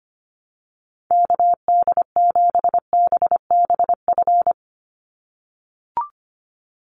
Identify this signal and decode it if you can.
Morse code, heard as a keyed tone, about 25 wpm, 700 Hz. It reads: KD7BBF
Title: USA Call Signs 25wpm